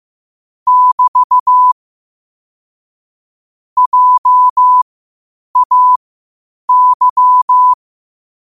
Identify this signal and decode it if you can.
Morse code, heard as a keyed tone, about 15 wpm, spaced as for 8 wpm, 1000 Hz. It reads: <BT> JAY
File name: qotdfiles / 8wpm